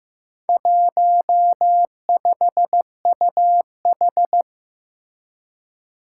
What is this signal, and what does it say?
Morse code, heard as a keyed tone, about 15 wpm, 700 Hz. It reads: 15UH
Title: Four Challenging Letter-Number Combinations 15wpm